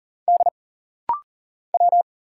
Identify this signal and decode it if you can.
Morse code, heard as a keyed tone, about 40 wpm, 700 Hz. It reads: D W
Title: Single Letters - Rapid-Fire 40wpm